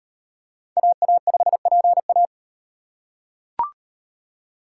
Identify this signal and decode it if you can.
Morse code, heard as a keyed tone, about 38 wpm, 700 Hz. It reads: AA5PA